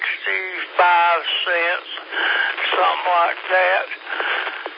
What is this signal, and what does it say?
Emergency Alert System (SAME) tones (header bursts), heard as harsh buzzing bursts, with speech. It is heard from 0.2 to 4.5 s.